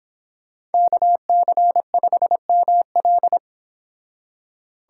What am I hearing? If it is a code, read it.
Morse code, heard as a keyed tone, about 30 wpm, 690 Hz. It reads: KC5ML